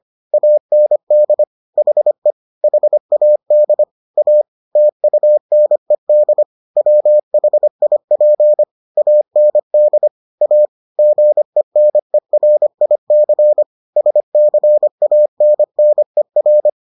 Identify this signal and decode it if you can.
Morse code, heard as a keyed tone, about 25 wpm, 600 Hz. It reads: AND HE HAD A TUNED WHIP AND A GENERIC SCANNER